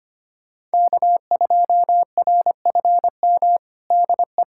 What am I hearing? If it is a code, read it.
Morse code, heard as a keyed tone, about 25 wpm, 700 Hz. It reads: K2RFM DE
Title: USA Callsign DE Callsign - 5 Characters - 25wpm